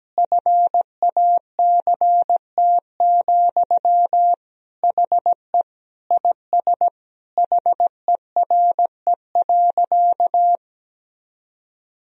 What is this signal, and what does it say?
Morse code, heard as a keyed tone, about 17 wpm, 700 Hz. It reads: FACT, HE IS HERE.
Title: Sentences from Top 1000 Words - Review Part 3 - 17wpm